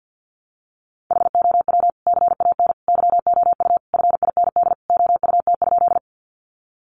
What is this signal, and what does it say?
Morse code, heard as a keyed tone, about 51 wpm, 700 Hz. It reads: HOW CAN YOU FIND OUT?
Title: Sentences from Top 200 Words 50wpm